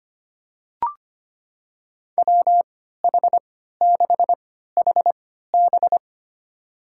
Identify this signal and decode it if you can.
Morse code, heard as a keyed tone, about 25 wpm, spaced as for 14 wpm, 700 Hz. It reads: WH6HB
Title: USA Call Signs - 5 Characters - 3x Character Spacing - 25wpm